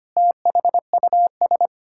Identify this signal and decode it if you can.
Morse code, heard as a keyed tone, about 25 wpm, 700 Hz. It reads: THUS
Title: Pangram Challenge v2 - 25wpm